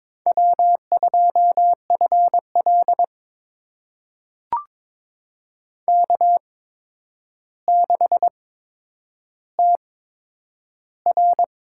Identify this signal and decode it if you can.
Morse code, heard as a keyed tone, about 22 wpm, spaced as for 9 wpm, 700 Hz. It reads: W2FL K6TR
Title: USA Call Signs - 4 Characters - 8x Character Spacing - 22wpm